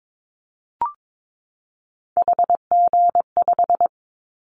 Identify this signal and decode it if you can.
Morse code, heard as a keyed tone, about 22 wpm, 700 Hz. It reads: HG5
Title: Callsign Prefixes for Top 25 DXCC - 22wpm